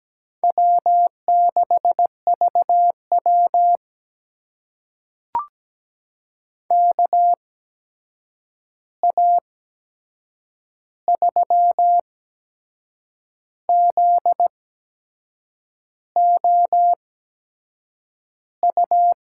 Morse code, heard as a keyed tone, about 17 wpm, spaced as for 6 wpm, 700 Hz. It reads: W6VW KA3ZOU
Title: USA Call Signs - 8x Character Spacing - 17wpm